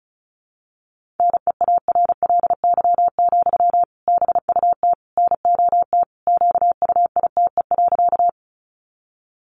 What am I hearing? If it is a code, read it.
Morse code, heard as a keyed tone, about 34 wpm, 700 Hz. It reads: NEARLY, BUT NOT QUITE.
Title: Sentences from Top 1300 Words 35wpm